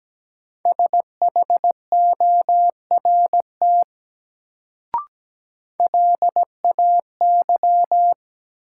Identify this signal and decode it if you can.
Morse code, heard as a keyed tone, about 17 wpm, 700 Hz. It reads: SHORT LAY
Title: Sentences from Top 400 Words - New Word Drill - 17wpm